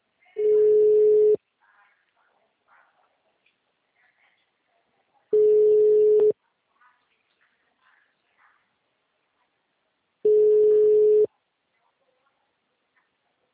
Ringback tone (Europe, 425 Hz).